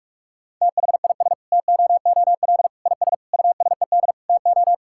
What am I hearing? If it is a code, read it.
Morse code, heard as a keyed tone, about 46 wpm, 700 Hz. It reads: THIS TOOL IS USED TO
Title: Sentences from Top 1000 Words - Review Part 6 - 45wpm